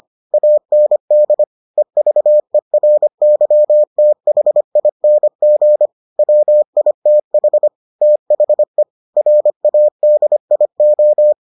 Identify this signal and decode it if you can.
Morse code, heard as a keyed tone, about 25 wpm, 600 Hz. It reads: AND EVERYTHING WITH THE RADIO